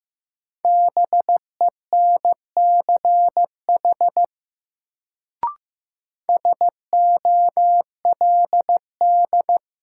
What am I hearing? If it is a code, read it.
Morse code, heard as a keyed tone, about 15 wpm, 700 Hz. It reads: BENCH SOLD
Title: Sentences from Top 1400 Words - New Word Drill Alt - 15wpm